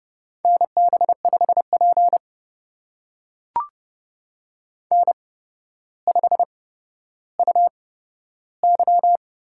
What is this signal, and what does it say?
Morse code, heard as a keyed tone, about 30 wpm, spaced as for 13 wpm, 700 Hz. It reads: NB5P N5UY
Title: USA Call Signs - 4 Characters - 8x Character Spacing - 30wpm